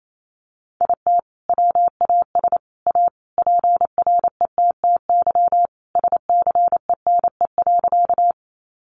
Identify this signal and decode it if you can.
Morse code, heard as a keyed tone, about 28 wpm, 700 Hz. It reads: IT WAS A PRETTY SCENE.
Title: Sentences from Top 1100 Words 28wpm